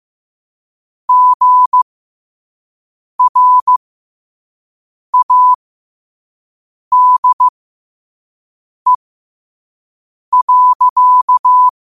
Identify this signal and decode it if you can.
Morse code, heard as a keyed tone, about 15 wpm, spaced as for 5 wpm, 1000 Hz. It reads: GRADE.